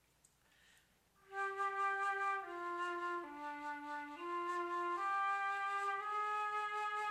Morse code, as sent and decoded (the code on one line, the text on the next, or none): none